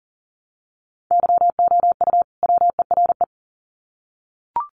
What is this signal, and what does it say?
Morse code, heard as a keyed tone, about 40 wpm, 700 Hz. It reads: YOU WERE
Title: Sets of 2 Words 40wpm